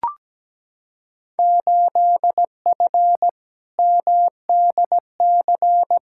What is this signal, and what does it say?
Morse code, heard as a keyed tone, about 17 wpm, 700 Hz. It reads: 8F MDC
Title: Field Day Exchanges Encore - 17wpm